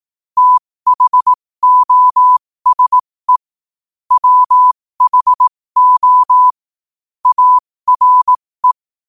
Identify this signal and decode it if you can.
Morse code, heard as a keyed tone, about 18 wpm, spaced as for 15 wpm, 1000 Hz. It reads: THOSE WHO ARE
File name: qotdfiles / 15wpm